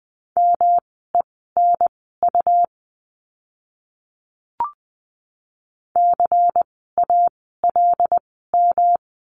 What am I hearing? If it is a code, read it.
Morse code, heard as a keyed tone, about 20 wpm, spaced as for 14 wpm, 700 Hz. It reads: MENU CALM
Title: 108 Lesson 18f - Letter m - Words 20wpm